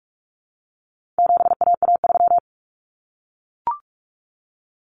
Morse code, heard as a keyed tone, about 44 wpm, 700 Hz. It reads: 7AA3
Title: Common Cut Numbers - 4-Digit 45wpm